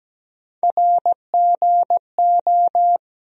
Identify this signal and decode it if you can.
Morse code, heard as a keyed tone, about 17 wpm, 700 Hz. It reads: RGO
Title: USA Call Sign Postfixes - 3 Letters 17wpm